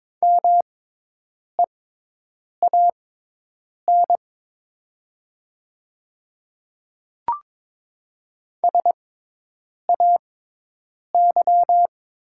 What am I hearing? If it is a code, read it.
Morse code, heard as a keyed tone, about 22 wpm, spaced as for 8 wpm, 700 Hz. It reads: MEAN SAY
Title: 6x Character Spacing - Top 300 Words - 22wpm